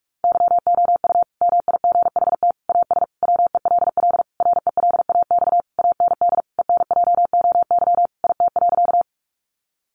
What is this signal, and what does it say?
Morse code, heard as a keyed tone, about 44 wpm, 700 Hz. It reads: YOU MIGHT AS WELL RELAX AND ENJOY IT.
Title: Sentences from Top 1400 Words 45wpm